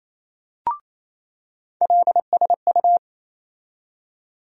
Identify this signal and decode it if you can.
Morse code, heard as a keyed tone, about 28 wpm, 700 Hz. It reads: LSU
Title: USA Call Sign Postfixes - 3 Letters 28wpm